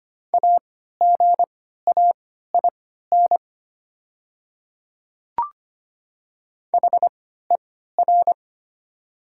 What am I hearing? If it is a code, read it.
Morse code, heard as a keyed tone, about 25 wpm, spaced as for 14 wpm, 700 Hz. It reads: AGAIN HER
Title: 3x Character Spacing - Top 500 Words - 25wpm